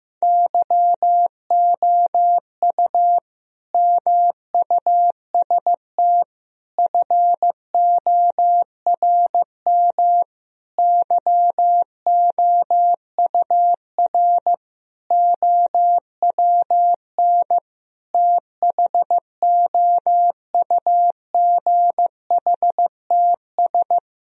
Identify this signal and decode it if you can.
Morse code, heard as a keyed tone, about 16 wpm, 700 Hz. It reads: YOU MUST FORM YOUR OWN THOUGHTS
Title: Sentences from Top 1000 Words - Review Part 1 - 15wpm